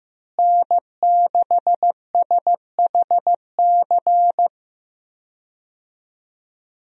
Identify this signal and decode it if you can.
Morse code, heard as a keyed tone, about 16 wpm, 700 Hz. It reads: N6SHC
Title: USA Call Signs - 5 Characters - Repeated 3x - 15wpm